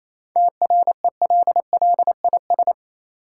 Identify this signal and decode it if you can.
Morse code, heard as a keyed tone, about 27 wpm, 700 Hz. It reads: TRELLIS
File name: Easy Words - Speed-Racing with Hints - 22wpm